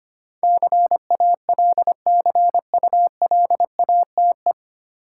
Morse code, heard as a keyed tone, about 25 wpm, 690 Hz. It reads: CALCULATE
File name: practice / 25wpm / 9 Letter Words - First 3 Letters Given - 25wpm